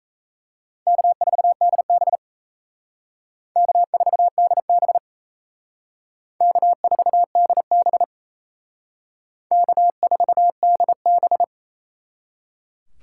Morse code, heard as a keyed tone, about 35 wpm, 700 Hz. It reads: K4DB K4DB K4DB K4DB
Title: Speed Racing - USA 1x2 Call Signs 28wpm